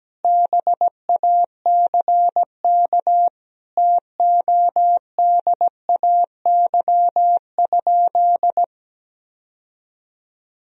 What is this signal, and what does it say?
Morse code, heard as a keyed tone, about 17 wpm, 700 Hz. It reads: BACK TODAY?
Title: Sentences from Top 1000 Words - Review Part 4 - 17wpm